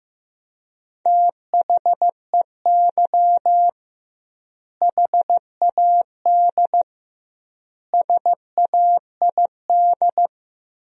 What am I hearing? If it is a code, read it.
Morse code, heard as a keyed tone, about 15 wpm, 700 Hz. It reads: THEY HAD SAID
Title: 2x Word-Spacing - Sentences from Top 400 Words - 15wpm